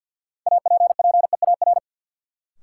Morse code, heard as a keyed tone, about 53 wpm, 700 Hz. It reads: APPEAR
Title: Sentences from Top 1000 Words - Review Part 2 - New Word Drill - 50wpm